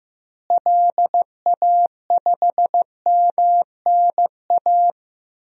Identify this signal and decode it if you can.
Morse code, heard as a keyed tone, about 16 wpm, 700 Hz. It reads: LA5MNA
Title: DXCC-100 Call Signs - 6 Characters - Repeated 2x - 15wpm